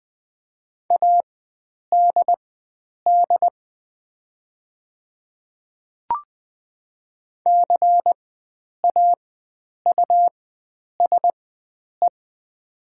Morse code, heard as a keyed tone, about 20 wpm, spaced as for 9 wpm, 700 Hz. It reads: ADD CAUSE
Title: 4x Character Spacing - Top 300 Words - 20wpm